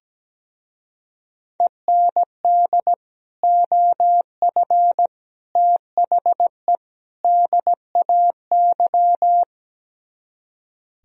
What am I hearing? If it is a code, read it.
Morse code, heard as a keyed tone, about 17 wpm, 700 Hz. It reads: END OF THE DAY
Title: Speed-Racing - Sets of 4 Words 17wpm